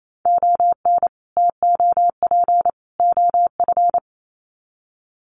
Morse code, heard as a keyed tone, about 28 wpm, 700 Hz. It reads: ON TOP OF
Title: Sets of 3 Words 28wpm